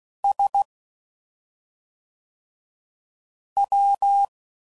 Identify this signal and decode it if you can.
Morse code, heard as a keyed tone, about 16 wpm, 800 Hz. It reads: S W